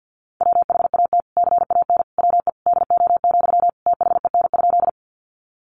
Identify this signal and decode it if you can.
Morse code, heard as a keyed tone, about 50 wpm, 700 Hz. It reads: WHAT CAN WE DO, THEN?
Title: Sentences from Top 100 Words 50wpm